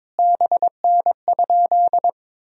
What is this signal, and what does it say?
Morse code, heard as a keyed tone, about 22 wpm, 700 Hz. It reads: BN?
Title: Sentences from Top 200 Words and Abbreviations 22wpm